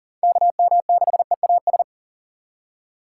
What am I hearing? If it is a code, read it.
Morse code, heard as a keyed tone, about 40 wpm, 700 Hz. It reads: KM6EAS
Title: USA Call Signs - 6 Characters - 6x Character Spacing - 40wpm